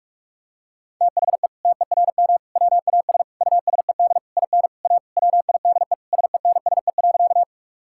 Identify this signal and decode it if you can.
Morse code, heard as a keyed tone, about 46 wpm, 700 Hz. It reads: THE TERM WAS USED IN A WIDE SENSE.
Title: Sentences from Top 1000 Words - Review Part 6 - 45wpm